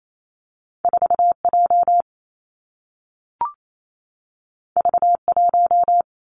Morse code, heard as a keyed tone, about 28 wpm, 700 Hz. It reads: 4 J V 1
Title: Two Challenging Letter-Number Combinations 28wpm